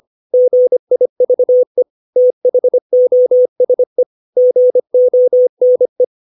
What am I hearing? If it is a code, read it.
Morse code, heard as a keyed tone, about 25 wpm, 500 Hz. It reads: GIVE THOSE GONE
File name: 632 words / 632 Words 25WPM